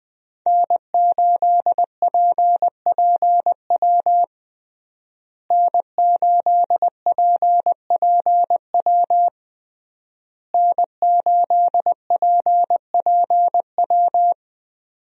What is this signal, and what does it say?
Morse code, heard as a keyed tone, about 20 wpm, 700 Hz. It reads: N8PPW N8PPW N8PPW
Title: USA Call Signs - 5 Characters - Repeated 3x - 20wpm